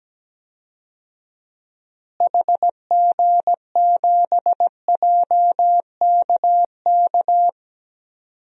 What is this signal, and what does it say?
Morse code, heard as a keyed tone, about 17 wpm, 700 Hz. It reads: HG7JKK
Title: DXCC-100 Call Signs - 6 Characters - Repeated 3x - 17wpm